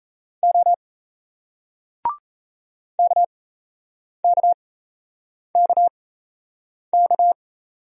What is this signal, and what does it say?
Morse code, heard as a keyed tone, about 32 wpm, spaced as for 8 wpm, 700 Hz. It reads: O KKKK
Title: Speed-Racing Single Letters 28wpm